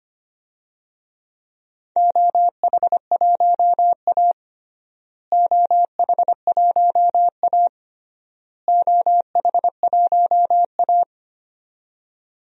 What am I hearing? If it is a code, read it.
Morse code, heard as a keyed tone, about 25 wpm, 700 Hz. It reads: OH1A OH1A OH1A
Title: DXCC-100 Call Signs - 4 Characters - Repeated 3x - 25wpm